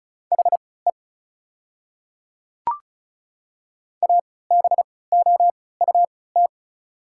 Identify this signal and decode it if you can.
Morse code, heard as a keyed tone, about 34 wpm, spaced as for 20 wpm, 690 Hz. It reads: HE ABOUT
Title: 3x Character Spacing - Top 100 Words - 35wpm